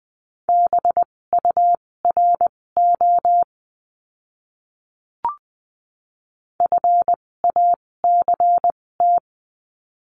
Morse code, heard as a keyed tone, about 20 wpm, spaced as for 16 wpm, 700 Hz. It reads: BURO FACT
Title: 256 Lesson 42a - Words 12-20wpm